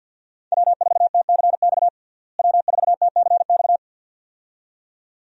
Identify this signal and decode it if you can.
Morse code, heard as a keyed tone, about 51 wpm, 700 Hz. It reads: W4TCX W4TCX
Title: USA Call Signs - 5 Characters - Repeated 2x - 50wpm